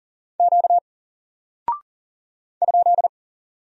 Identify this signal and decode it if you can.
Morse code, heard as a keyed tone, about 41 wpm, 700 Hz. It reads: Q ?
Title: Single Character (Letter:Number:Punctuation:Prosign) Alt 40wpm